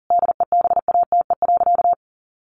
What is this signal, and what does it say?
Morse code, heard as a keyed tone, about 40 wpm, 700 Hz. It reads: DEBATE.
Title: Sentences from Top 1100 Words 40wpm